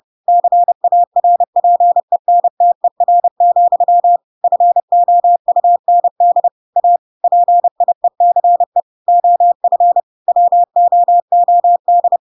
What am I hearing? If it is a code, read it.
Morse code, heard as a keyed tone, about 30 wpm, 700 Hz. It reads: CARPENTER, FOUND A PIECE OF WOOD
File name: Pinocchio / Pinocchio ch1 30 30 ik1hge